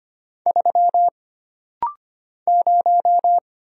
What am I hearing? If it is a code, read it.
Morse code, heard as a keyed tone, about 25 wpm, 700 Hz. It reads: 3 0
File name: Single Character (Letter:Number:Punctuation:Prosign) - Rapid-Fire - 25wpm